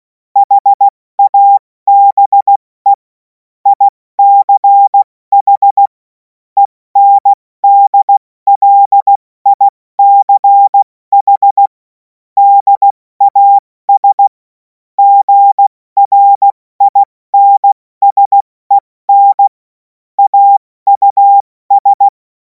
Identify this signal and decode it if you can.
Morse code, heard as a keyed tone, about 16 wpm, 800 Hz. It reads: HABE ICH ENDLICH DAS GRINSEN AUS
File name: Morsekurs / Witze / Witz 11. August 2026 70 F80